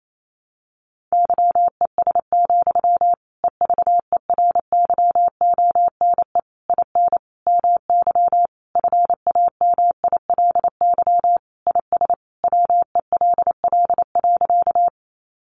Morse code, heard as a keyed tone, about 28 wpm, 700 Hz. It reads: YES, EVERYONE IN MY FAMILY IS WELL.
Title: Sentences from Top 800 Words 28wpm